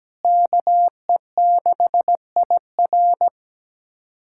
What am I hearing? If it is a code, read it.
Morse code, heard as a keyed tone, about 17 wpm, 700 Hz. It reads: KE6IR
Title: SOTA QSO Practice 17wpm